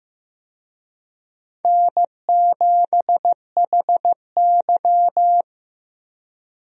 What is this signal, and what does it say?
Morse code, heard as a keyed tone, about 15 wpm, 700 Hz. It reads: N7HY
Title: USA Call Signs - 4 Characters - Repeated 2x - 15wpm